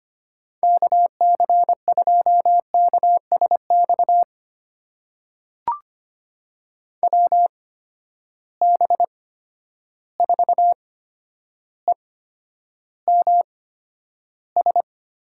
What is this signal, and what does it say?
Morse code, heard as a keyed tone, about 25 wpm, spaced as for 11 wpm, 700 Hz. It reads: KC2KSX WB4EMS
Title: USA Call Signs - 6 Characters - 8x Character Spacing - 25wpm